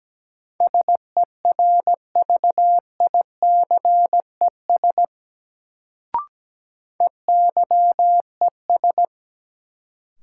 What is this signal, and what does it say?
Morse code, heard as a keyed tone, about 17 wpm, 700 Hz. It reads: SERVICES EYES
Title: Sets of 5 Words - New Word Drill - 17wpm